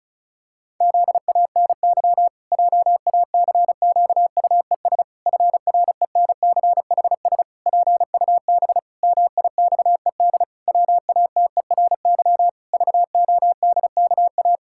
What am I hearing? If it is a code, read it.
Morse code, heard as a keyed tone, about 34 wpm, 700 Hz. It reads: ZANY JACQUES FRENCHS PUB MIXED WATERY VODKA
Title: Pangram Challenge v2 - 35wpm